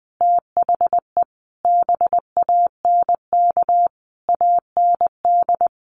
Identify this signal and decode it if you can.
Morse code, heard as a keyed tone, about 19 wpm, 700 Hz. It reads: THE BANK AND
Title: Sentences from Top 1500 Words 20wpm